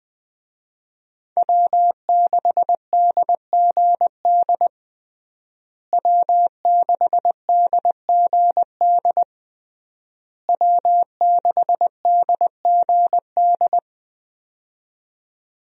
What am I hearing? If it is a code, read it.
Morse code, heard as a keyed tone, about 20 wpm, 700 Hz. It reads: W6DGD W6DGD W6DGD
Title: USA Call Signs - 5 Characters - Repeated 3x - 20wpm